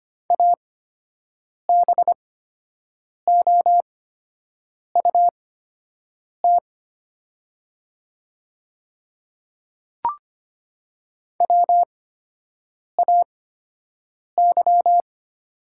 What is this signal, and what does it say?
Morse code, heard as a keyed tone, about 25 wpm, spaced as for 7 wpm, 700 Hz. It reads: ABOUT WAY
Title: 8x Character Spacing - Top 100 Words - 25wpm